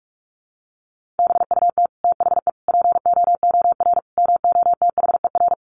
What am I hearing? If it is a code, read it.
Morse code, heard as a keyed tone, about 44 wpm, 700 Hz. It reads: BUT THE POOR MOTHER
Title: Sentences from Top 1000 Words 45wpm